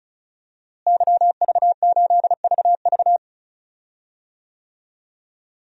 Morse code, heard as a keyed tone, about 34 wpm, 700 Hz. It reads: YV8VV
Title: DXCC-100 Call Signs - 5 Characters - Repeated 2x - 35wpm